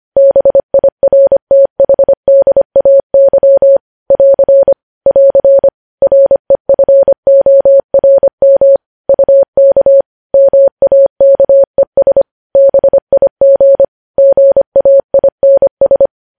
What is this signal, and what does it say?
Morse code, heard as a keyed tone, about 25 wpm, 550 Hz. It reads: BIRTHDAY <AR> <AR> REFORM UK MAKES BIG GAINS
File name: podcast / News headlines-1x-25x25